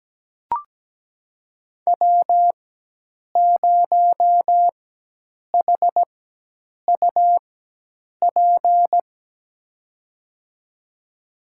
Morse code, heard as a keyed tone, about 17 wpm, spaced as for 8 wpm, 700 Hz. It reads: W0HUP
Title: USA Call Signs - 5 Characters - 4x Character Spacing - 17wpm